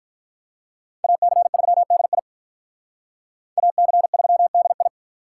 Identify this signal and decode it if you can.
Morse code, heard as a keyed tone, about 45 wpm, 700 Hz. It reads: AC3DI AC3DI